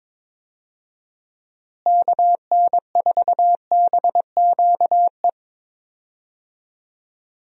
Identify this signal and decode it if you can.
Morse code, heard as a keyed tone, about 22 wpm, 700 Hz. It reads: KN4BQE
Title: USA Call Signs - 6 Characters - Repeated 2x - 22wpm